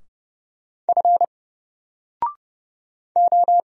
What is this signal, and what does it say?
Morse code, heard as a keyed tone, about 30 wpm, 700 Hz. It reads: F O
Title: Single Character (Letter:Number:Punctuation:Prosign) 30wpm